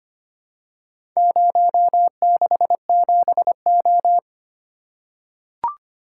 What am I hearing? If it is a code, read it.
Morse code, heard as a keyed tone, about 25 wpm, 700 Hz. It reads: 067O
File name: practice / 25wpm / Four Letter-Number Combinations - 25wpm